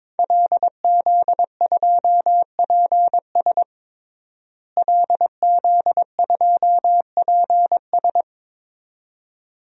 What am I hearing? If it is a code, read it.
Morse code, heard as a keyed tone, about 22 wpm, 700 Hz. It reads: LZ2PS LZ2PS
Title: DXCC-100 Call Signs - 5 Characters - Repeated 2x - 22wpm